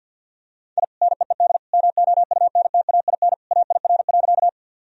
Morse code, heard as a keyed tone, about 46 wpm, 700 Hz. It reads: I NEED MOUNTAIN AIR.
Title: Sentences from Top 1000 Words - Review Part 1 - 50wpm